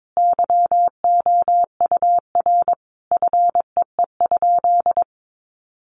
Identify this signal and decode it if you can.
Morse code, heard as a keyed tone, about 22 wpm, 700 Hz. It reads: YOUR FEE?
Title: Sentences from Top 1500 Words 22wpm